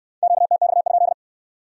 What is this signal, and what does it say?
Morse code, heard as a keyed tone, about 65 wpm, 700 Hz. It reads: QTX?